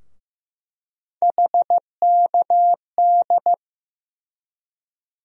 Morse code, heard as a keyed tone, about 15 wpm, 700 Hz. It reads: HKD
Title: USA Call Sign Postfixes - 3 Letters 15wpm